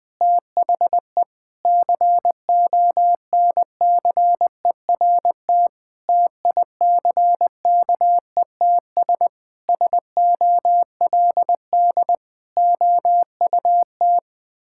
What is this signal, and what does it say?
Morse code, heard as a keyed tone, about 20 wpm, 700 Hz. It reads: THE CONCERT TICKETS SOLD OUT
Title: Sentences with 6-Word Phrases - 20wpm